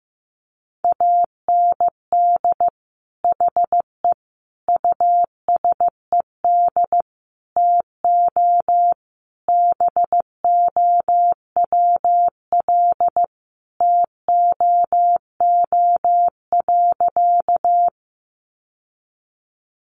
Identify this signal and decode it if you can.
Morse code, heard as a keyed tone, about 14 wpm, 700 Hz. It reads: AND HE USED TO BOWL TOO.
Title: Sentences from Top 1200 Words 15wpm